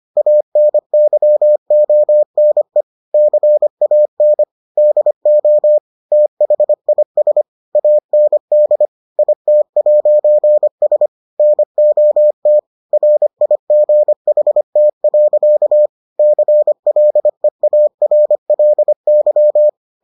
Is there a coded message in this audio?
Yes: Morse code from 0.2 to 19.7 s, heard as a keyed tone, about 25 wpm, 600 Hz. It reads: ANYONE CAN DO THIS AND IT'S NOT RIGHT. CLEARLY